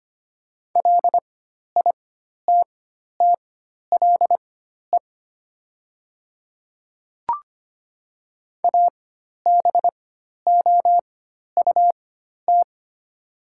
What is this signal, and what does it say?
Morse code, heard as a keyed tone, about 25 wpm, spaced as for 12 wpm, 700 Hz. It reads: LITTLE ABOUT
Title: 4x Character Spacing - Top 200 Words - 25wpm